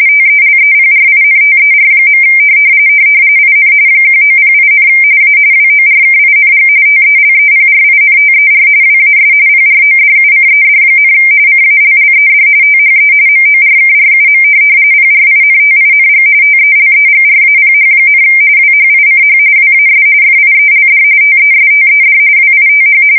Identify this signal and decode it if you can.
RTTY, about 45.45 baud. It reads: TELEPRINTER. IN 1908, A WORKING TELEPRINTER WAS PRODUCED, CALLED THE MORKRUM PRINTING TELEGRAPH, WHICH WAS FIELD TESTED WITH
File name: fsk170 / teletype-corp